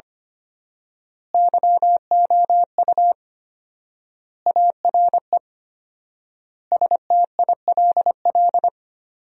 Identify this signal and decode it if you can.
Morse code, heard as a keyed tone, about 25 wpm, 700 Hz. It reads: YOU ARE STILL
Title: 4x Word-Spacing - Sentences from Top 400 Words - 25wpm